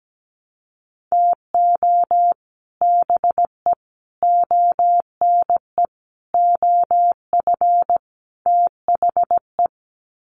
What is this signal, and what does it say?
Morse code, heard as a keyed tone, about 17 wpm, 700 Hz. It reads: TO BE ONE OF THE
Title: Sets of 5 Words 17wpm